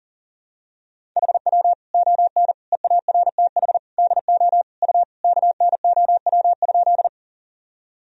Morse code, heard as a keyed tone, about 37 wpm, 700 Hz. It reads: HW ON EARTH DO U KNOW?
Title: Sentences from Top 200 Words and Abbreviations 40wpm